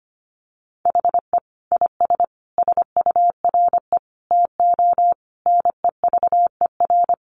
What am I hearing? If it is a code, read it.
Morse code, heard as a keyed tone, about 25 wpm, 700 Hz. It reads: HE IS SURE TO NEVER
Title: Sentences from Top 700 Words 25wpm